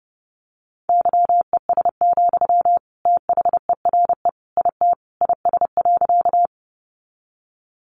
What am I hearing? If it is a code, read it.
Morse code, heard as a keyed tone, about 28 wpm, 700 Hz. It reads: YES, THERE IT IS.
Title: Sentences from Top 500 Words 30wpm